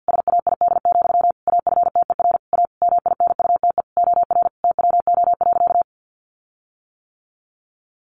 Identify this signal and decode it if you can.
Morse code, heard as a keyed tone, about 50 wpm, 700 Hz. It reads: SAID, AFTER A MINUTE OR TWO.
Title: Sentences from Top 1400 Words 50wpm